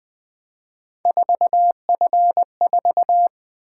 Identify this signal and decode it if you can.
Morse code, heard as a keyed tone, about 20 wpm, 700 Hz. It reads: 4F4
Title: Three Letter-Number Combinations - 20wpm